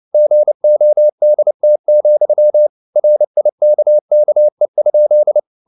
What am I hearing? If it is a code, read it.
Morse code, heard as a keyed tone, about 29 wpm, 600 Hz. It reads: GODT, RIKKE?
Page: CW med Gnister